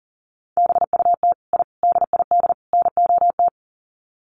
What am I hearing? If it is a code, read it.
Morse code, heard as a keyed tone, about 40 wpm, 700 Hz. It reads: BUT I DID NOT
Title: Sets of 4 Words 40wpm